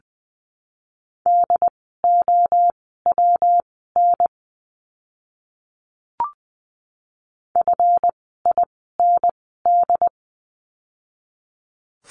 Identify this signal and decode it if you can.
Morse code, heard as a keyed tone, about 20 wpm, spaced as for 14 wpm, 700 Hz. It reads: DOWN FIND
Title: 139 Lesson 23f - Letter f - Words 20wpm